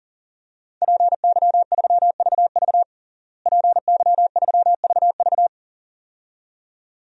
Morse code, heard as a keyed tone, about 37 wpm, 700 Hz. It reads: PY3VV PY3VV
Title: DXCC-100 Call Signs - 5 Characters - Repeated 2x - 40wpm